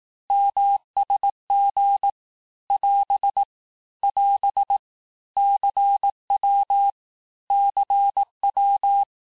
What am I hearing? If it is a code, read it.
Morse code, heard as a keyed tone, about 18 wpm, 800 Hz. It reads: MSG <AS> <AS> CW CW
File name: cours-du-lundi files / cours-1588-texte-abreviations-vitesse-17mpm-puis-22mpm